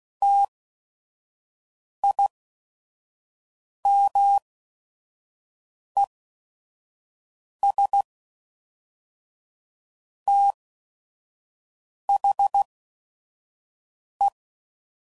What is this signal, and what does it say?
Morse code, heard as a keyed tone, about 16 wpm, spaced as for 5 wpm, 800 Hz. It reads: TIMES THE
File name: CW Code Groups @5WPM / Technical Subjects / XL